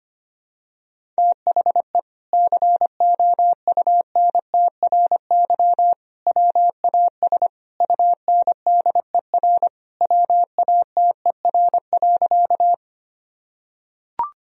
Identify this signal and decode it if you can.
Morse code, heard as a keyed tone, about 25 wpm, 700 Hz. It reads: THE COUNTRY WAS UNDER WATER.
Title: Sentences from Top 1000 Words - Review Part 1 - 25wpm